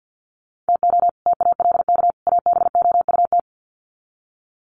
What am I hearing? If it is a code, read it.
Morse code, heard as a keyed tone, about 50 wpm, 700 Hz. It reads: TO TALK ABOUT
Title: Sets of 3 Words 50wpm